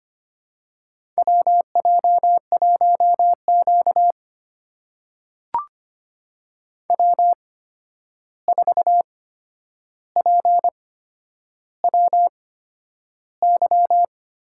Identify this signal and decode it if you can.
Morse code, heard as a keyed tone, about 25 wpm, spaced as for 10 wpm, 700 Hz. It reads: WJ1Q W4PWY
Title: USA Call Signs - 8x Character Spacing - 25wpm